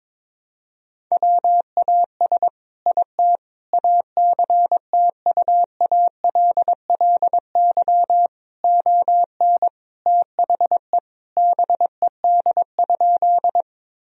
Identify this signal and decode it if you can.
Morse code, heard as a keyed tone, about 22 wpm, 700 Hz. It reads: WAS IT ACTUALLY ON THE BED?
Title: Sentences from Top 1000 Words - Review Part 4 - 22wpm